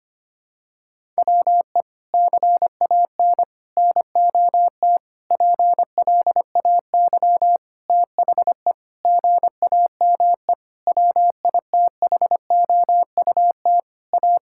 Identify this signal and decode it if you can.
Morse code, heard as a keyed tone, about 25 wpm, 700 Hz. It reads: WE CAN NOT PLAY THE GAME WITHOUT A
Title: Sentences from Top 1000 Words - Review Part 6 - 25wpm